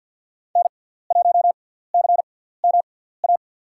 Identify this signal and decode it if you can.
Morse code, heard as a keyed tone, about 52 wpm, spaced as for 17 wpm, 700 Hz. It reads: N1CMA
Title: USA Call Signs - 5 Characters - 6x Character Spacing - 50wpm